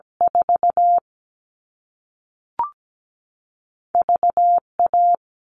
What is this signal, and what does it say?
Morse code, heard as a keyed tone, about 17 wpm, 700 Hz. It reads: 4 V A
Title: All Cut Numbers - 2-Digit 17wpm